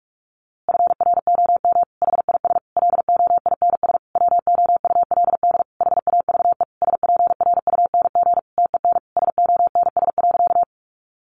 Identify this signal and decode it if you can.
Morse code, heard as a keyed tone, about 44 wpm, 700 Hz. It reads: FROM HIS LOINS WOULD HAVE SPRUNG TEN SONS.
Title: A Sound Of Thunder 45wpm